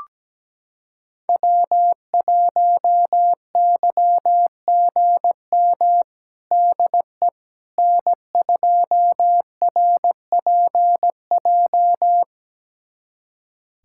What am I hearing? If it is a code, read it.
Morse code, heard as a keyed tone, about 17 wpm, 700 Hz. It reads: W1YGM DE N2RPJ
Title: USA Callsign DE Callsign - 5 Characters - 17wpm